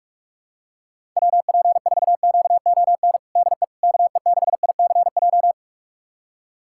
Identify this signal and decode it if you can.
Morse code, heard as a keyed tone, about 41 wpm, 700 Hz. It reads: WP4QON DE KE6ICJ
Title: USA Callsign DE Callsign 45wpm